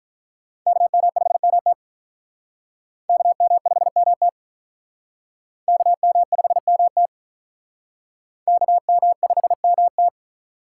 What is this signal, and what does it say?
Morse code, heard as a keyed tone, about 45 wpm, 700 Hz. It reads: KM5MT KM5MT KM5MT KM5MT